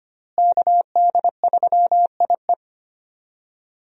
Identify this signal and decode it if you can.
Morse code, heard as a keyed tone, about 25 wpm, 700 Hz. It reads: KD3IE
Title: USA Call Signs - 5 Characters - Repeated 3x - 25wpm